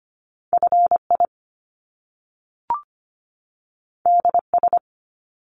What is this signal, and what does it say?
Morse code, heard as a keyed tone, about 25 wpm, 700 Hz. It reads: FI DS
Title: Top Two Letter Combinations in Words 25wpm